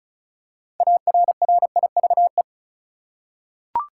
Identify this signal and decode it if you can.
Morse code, heard as a keyed tone, about 36 wpm, 700 Hz. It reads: ARRIVE
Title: Six Letter Words - 35wpm